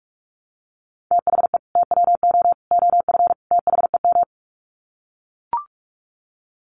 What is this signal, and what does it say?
Morse code, heard as a keyed tone, about 44 wpm, 700 Hz. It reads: THE TWO OF THEM
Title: Sets of 4 Words 45wpm